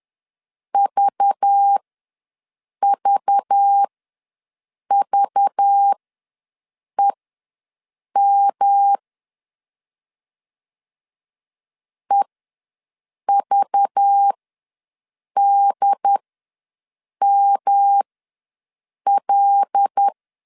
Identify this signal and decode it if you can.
Morse code, heard as a keyed tone, about 10 wpm, spaced as for 6 wpm, 800 Hz. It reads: VVVEM EVDML
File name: LEZIONI / 5 LBVC / 5 LBV ws 10wpm 3645